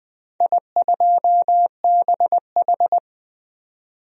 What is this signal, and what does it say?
Morse code, heard as a keyed tone, about 20 wpm, 700 Hz. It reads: I2BH
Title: DXCC-100 Call Signs - 4 Characters - 20wpm